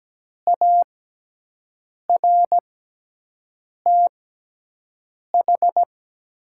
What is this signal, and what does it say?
Morse code, heard as a keyed tone, about 17 wpm, spaced as for 6 wpm, 700 Hz. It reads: ARTH
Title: 6x Character Spacing - Top 200 Words - 17wpm